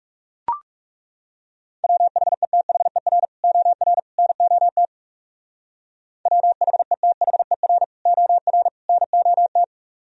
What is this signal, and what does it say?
Morse code, heard as a keyed tone, about 35 wpm, 700 Hz. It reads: WHETHER OR NOT WHETHER OR NOT
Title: Speed-Racing - Sets of 3 Words 30wpm